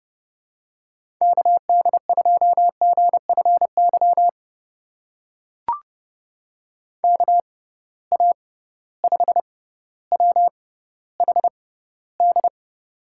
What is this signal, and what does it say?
Morse code, heard as a keyed tone, about 30 wpm, spaced as for 15 wpm, 700 Hz. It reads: KD2GFY KA5WHD
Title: USA Call Signs - 6 Characters - 6x Character Spacing - 30wpm